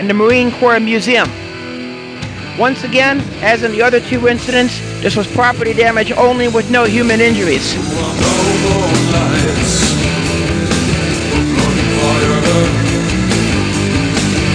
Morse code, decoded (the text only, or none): none